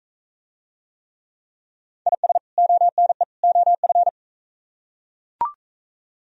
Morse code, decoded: IS ONE OF

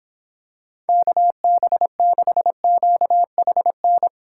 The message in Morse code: KB6QHN

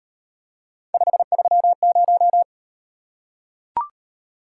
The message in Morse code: 530

